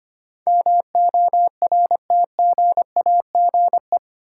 Morse code, heard as a keyed tone, about 25 wpm, 700 Hz. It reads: MORTGAGE
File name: Eight Letter Words - 25wpm